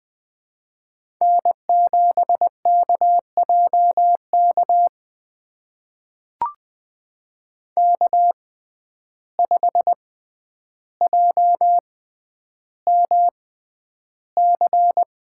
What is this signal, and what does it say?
Morse code, heard as a keyed tone, about 20 wpm, spaced as for 10 wpm, 700 Hz. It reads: N7KJK K5JMC
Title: USA Call Signs - 5 Characters - 6x Character Spacing - 20wpm